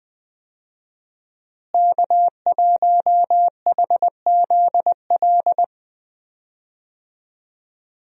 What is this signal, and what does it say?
Morse code, heard as a keyed tone, about 20 wpm, 700 Hz. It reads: K1HZL
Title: USA Call Signs - 5 Characters - Repeated 3x - 20wpm